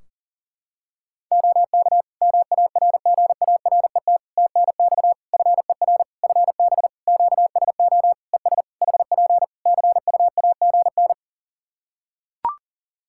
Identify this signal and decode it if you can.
Morse code, heard as a keyed tone, about 37 wpm, 700 Hz. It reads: OK MARGARET TNX FER FB QSO ES HP CUAGN
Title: Practice QSO - Exchange 4 - Ending - 40wpm